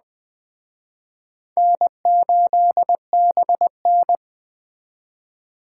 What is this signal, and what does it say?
Morse code, heard as a keyed tone, about 21 wpm, 700 Hz. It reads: N8BN
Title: Speed Racing - USA Call Signs 17wpm